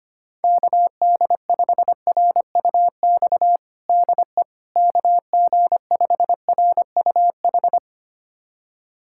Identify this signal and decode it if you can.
Morse code, heard as a keyed tone, about 25 wpm, 700 Hz. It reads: KD5RUX DE KG5RUH
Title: USA Callsign DE Callsign - 6 Characters - 25wpm